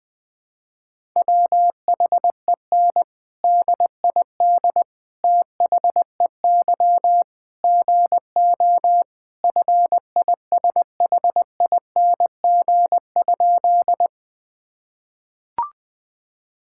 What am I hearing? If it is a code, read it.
Morse code, heard as a keyed tone, about 20 wpm, 700 Hz. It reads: WHEN DID THEY GO FISHING?
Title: Sentences from Top 1000 Words - Review Part 5 - 20wpm